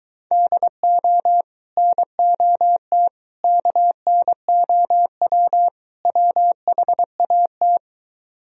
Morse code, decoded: DO NOT KNOW WHAT

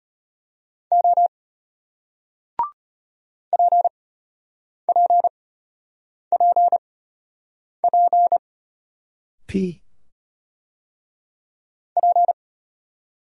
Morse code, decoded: O PPPP P

